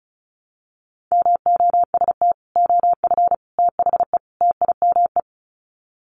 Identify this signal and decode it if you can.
Morse code, heard as a keyed tone, about 34 wpm, 700 Hz. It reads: MOST OF THE TIME
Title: Sets of 4 Words 35wpm